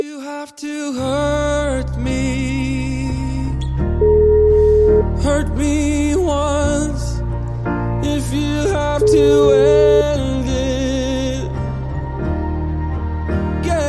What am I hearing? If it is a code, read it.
Ringback tone (Europe, 425 Hz), with noise.